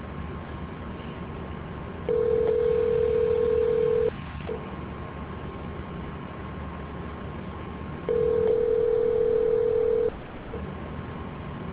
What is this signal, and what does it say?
Ringback tone (North America), 440 + 480 Hz, with noise.